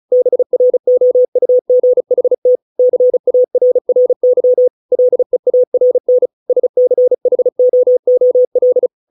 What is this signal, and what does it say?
Morse code, heard as a keyed tone, about 36 wpm, 500 Hz. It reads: BROUGHT CARRY LEARN SCHOOL